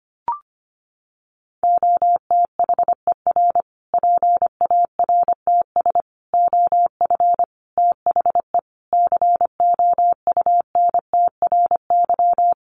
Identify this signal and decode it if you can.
Morse code, heard as a keyed tone, about 25 wpm, 700 Hz. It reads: OTHER PARTS OF THE COUNTRY
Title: Sets of 5 Words Encore 25wpm